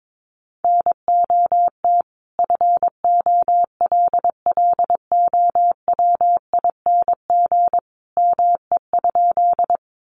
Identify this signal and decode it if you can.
Morse code, heard as a keyed tone, about 22 wpm, 700 Hz. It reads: NOT FOLLOWING ME?
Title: Sentences from Top 700 Words 22wpm